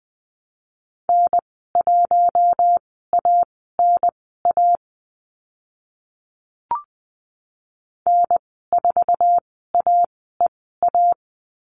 Morse code, heard as a keyed tone, about 20 wpm, spaced as for 14 wpm, 700 Hz. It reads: N1ANA N4AEA
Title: 062 Lesson 11g - Letter h - Call Signs 20wpm